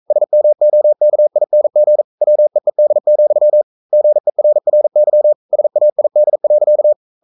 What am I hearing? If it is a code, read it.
Morse code, heard as a keyed tone, about 42 wpm, 600 Hz. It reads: SMOKING WEED, GERRY SAID.
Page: Romanen Never af Ken Follet på engelsk.